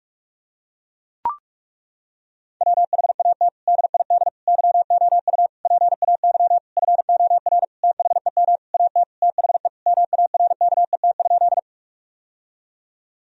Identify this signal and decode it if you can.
Morse code, heard as a keyed tone, about 42 wpm, 700 Hz. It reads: WHAT DID YOU PAY FOR THEM AT THE MARKET?
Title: Sentences from Top 1000 Words - Review Part 3 - 45wpm